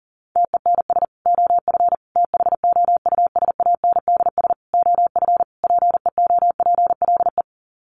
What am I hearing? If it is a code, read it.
Morse code, heard as a keyed tone, about 40 wpm, 700 Hz. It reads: TENS OF THOUSANDS OF PEOPLE
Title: Sets of 5 Words Encore2 40wpm